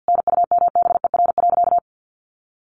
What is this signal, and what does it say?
Morse code, heard as a keyed tone, about 50 wpm, 700 Hz. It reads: NUMBER.
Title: Sentences from Top 1400 Words 50wpm